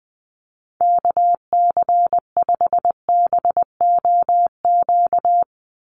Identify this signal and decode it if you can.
Morse code, heard as a keyed tone, about 20 wpm, 700 Hz. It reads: KC5BOQ
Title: USA Call Signs 20wpm